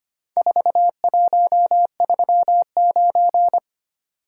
Morse code, heard as a keyed tone, about 25 wpm, 700 Hz. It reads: 4139
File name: practice / 25wpm / Four Letter-Number Combinations - 25wpm